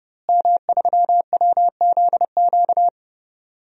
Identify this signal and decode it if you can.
Morse code, heard as a keyed tone, about 30 wpm, 700 Hz. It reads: M3WZQ
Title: DXCC-100 Call Signs - 5 Characters - Repeated 2x - 30wpm